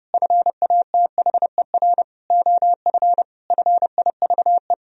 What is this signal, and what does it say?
Morse code, heard as a keyed tone, about 30 wpm, 700 Hz. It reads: FATHER OF FIVE
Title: Sentences from Top 1000 Words - Review Part 2 - 30wpm